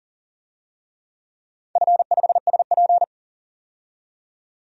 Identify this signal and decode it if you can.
Morse code, heard as a keyed tone, about 40 wpm, 700 Hz. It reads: F5SP
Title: DXCC-100 Call Signs - 4 Characters - 40wpm